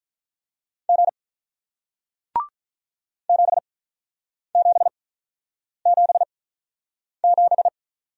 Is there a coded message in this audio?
Yes: Morse code from 0.9 to 7.7 s, heard as a keyed tone, about 48 wpm, spaced as for 9 wpm, 700 Hz. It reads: G 7777